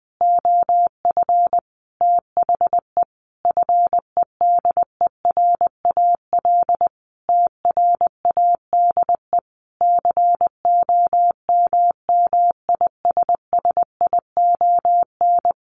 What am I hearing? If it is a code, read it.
Morse code, heard as a keyed tone, about 19 wpm, 700 Hz. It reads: OF THE FEDERAL TRADE COMMISSION
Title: Sentences from Top 1400 Words 20wpm